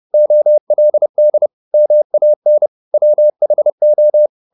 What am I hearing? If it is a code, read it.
Morse code, heard as a keyed tone, about 30 wpm, 600 Hz. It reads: OLD MAN WHO
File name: the old man and the sea / the old man and the sea 1 30wpm